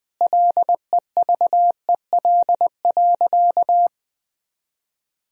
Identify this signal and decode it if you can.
Morse code, heard as a keyed tone, about 21 wpm, 700 Hz. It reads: LEVEL.
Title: Sentences from Top 1000 Words - Review Part 4 - 20wpm